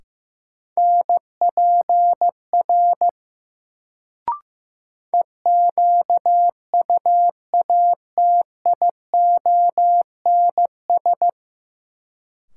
Morse code, heard as a keyed tone, about 15 wpm, 700 Hz. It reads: NPR EQUATIONS
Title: Sets of 5 Words - New Word Drill - 15wpm